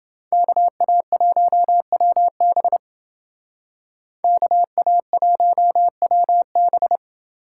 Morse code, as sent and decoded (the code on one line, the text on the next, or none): -.- .- .---- .-- -... / -.- .- .---- .-- -...
KA1WB KA1WB